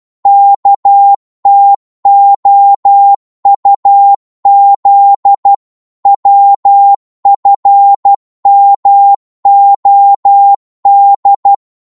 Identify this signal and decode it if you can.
Morse code, heard as a keyed tone, about 12 wpm, 800 Hz. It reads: KTOUZ WFMOD
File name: Morse / ltr-12wpm